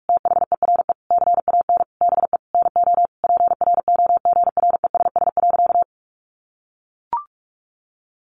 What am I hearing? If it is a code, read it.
Morse code, heard as a keyed tone, about 44 wpm, 700 Hz. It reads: THERE CAN BE NO PROGRESS.
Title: Sentences from Top 1400 Words 45wpm